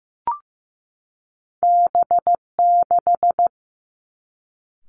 Morse code, heard as a keyed tone, about 15 wpm, 700 Hz. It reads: B6
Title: Two Challenging Letter-Number Combinations 15wpm